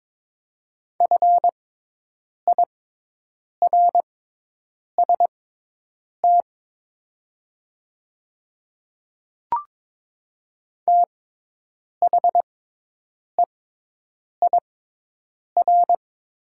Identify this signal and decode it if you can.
Morse code, heard as a keyed tone, about 22 wpm, spaced as for 8 wpm, 700 Hz. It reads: FIRST THEIR
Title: 6x Character Spacing - Top 100 Words - 22wpm